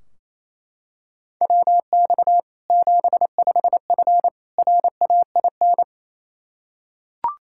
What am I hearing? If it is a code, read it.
Morse code, heard as a keyed tone, about 27 wpm, 700 Hz. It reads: WX 75F RAIN
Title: Practice QSO - Exchange 2 - Info - 28wpm